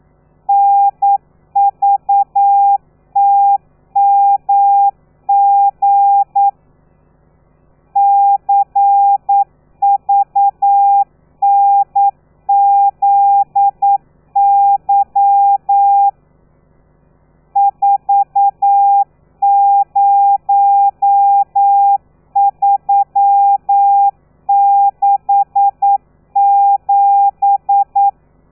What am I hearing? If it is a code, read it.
Morse code, heard as a keyed tone, about 9 wpm, 800 Hz. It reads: NVTMG CVNZY 40367
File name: telegrafia / i2tfj / Lez13 40